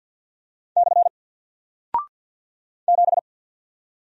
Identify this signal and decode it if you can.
Morse code, heard as a keyed tone, about 48 wpm, 700 Hz. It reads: / 7
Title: Single Character (Letter:Number:Punctuation:Prosign) Alt 50wpm